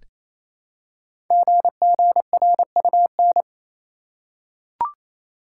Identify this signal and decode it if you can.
Morse code, heard as a keyed tone, about 27 wpm, 700 Hz. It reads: GGRUN
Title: Code Groups - 5 Letters - 28wpm